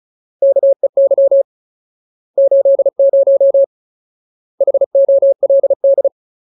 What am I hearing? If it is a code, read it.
Morse code, heard as a keyed tone, about 34 wpm, 550 Hz. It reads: KEY 80 HOLD
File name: Ultimate Word List / 35WPM Ultimate